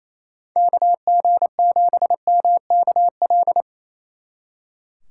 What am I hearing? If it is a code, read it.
Morse code, heard as a keyed tone, about 35 wpm, 700 Hz. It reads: KG7MKL
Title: Speed Racing - USA Call Signs 28wpm